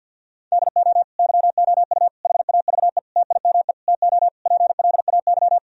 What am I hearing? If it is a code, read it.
Morse code, heard as a keyed tone, about 50 wpm, 700 Hz. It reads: DO YOU HAVE TIME TO PLAY
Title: Sentences from Top 1000 Words - Review Part 4 - 50wpm